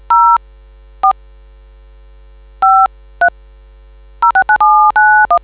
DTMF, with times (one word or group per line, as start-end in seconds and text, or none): 0.1-0.4 0
1.0-1.1 4
2.6-2.9 5
3.2-3.3 3
4.2-5.4 069791